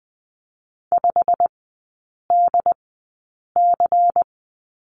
Morse code, heard as a keyed tone, about 20 wpm, spaced as for 8 wpm, 700 Hz. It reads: 5DC
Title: 127 Lesson 21e - Number 6 - Three Characters 20wpm